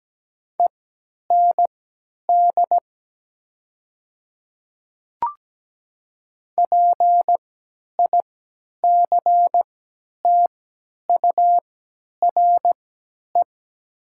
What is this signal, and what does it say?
Morse code, heard as a keyed tone, about 17 wpm, spaced as for 10 wpm, 700 Hz. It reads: END PICTURE
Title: 3x Character Spacing - Top 500 Words - 17wpm